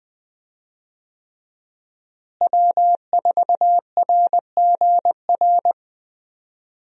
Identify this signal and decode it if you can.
Morse code, heard as a keyed tone, about 20 wpm, 700 Hz. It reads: W4RGR
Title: USA Call Signs - 5 Characters - Repeated 3x - 20wpm